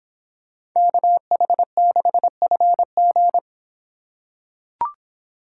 Morse code, heard as a keyed tone, about 26 wpm, 700 Hz. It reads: KH6FG